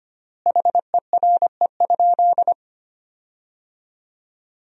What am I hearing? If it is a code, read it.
Morse code, heard as a keyed tone, about 25 wpm, 700 Hz. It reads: HERE?
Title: 4x Word-Spacing - Sentences from Top 300 Words - 25wpm